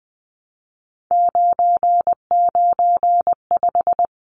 Morse code, heard as a keyed tone, about 20 wpm, 700 Hz. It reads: 995
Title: Numbers - 3-Digit 20wpm